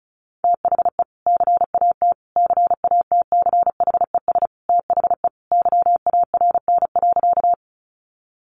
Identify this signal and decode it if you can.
Morse code, heard as a keyed tone, about 34 wpm, 700 Hz. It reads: THE CAT CATCHES THE YARN.
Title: Additional Easy Phrases 35wpm